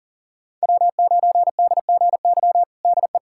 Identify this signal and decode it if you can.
Morse code, heard as a keyed tone, about 40 wpm, 700 Hz. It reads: W9DGY DE
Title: USA Callsign DE Callsign - 5 Characters - 40wpm